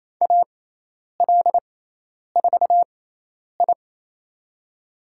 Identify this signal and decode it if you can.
Morse code, heard as a keyed tone, about 28 wpm, spaced as for 10 wpm, 700 Hz. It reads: AL4I
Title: USA Call Signs - 6x Character Spacing - 28wpm